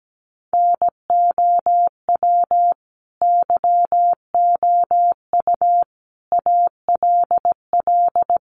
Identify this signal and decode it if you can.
Morse code, heard as a keyed tone, about 17 wpm, 700 Hz. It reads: NOW YOU ALL
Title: Sentences from Top 1200 Words 17wpm